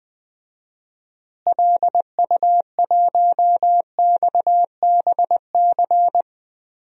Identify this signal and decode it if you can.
Morse code, heard as a keyed tone, about 20 wpm, 700 Hz. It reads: LU1XBC
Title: DXCC-100 Call Signs - 6 Characters - Repeated 3x - 20wpm